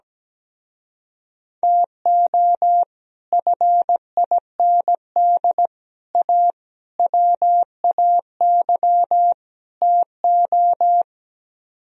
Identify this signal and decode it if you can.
Morse code, heard as a keyed tone, about 17 wpm, 700 Hz. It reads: TO FIND A WAY TO
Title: Speed-Racing - Sets of 5 Words 17wpm